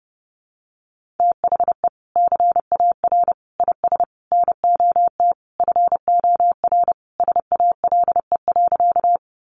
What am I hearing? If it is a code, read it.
Morse code, heard as a keyed tone, about 30 wpm, 700 Hz. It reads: THE CAR IS NOT FOR SALE.
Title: Sentences from Top 1400 Words 30wpm